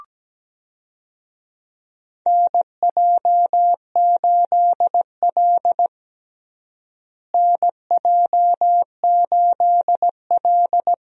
Morse code, heard as a keyed tone, about 17 wpm, 700 Hz. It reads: NJ8L NJ8L
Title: USA Call Signs - 4 Characters - Repeated 3x - 17wpm